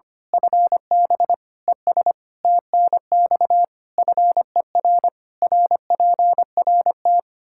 Morse code, heard as a keyed tone, about 25 wpm, 700 Hz. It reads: FB ES TNX FER RPRT
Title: Practice QSO - Exchange 2 - Info - 25wpm